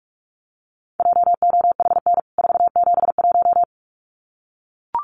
Morse code, heard as a keyed tone, about 44 wpm, 700 Hz. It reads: JOHN 471
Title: CWT Exchange 45wpm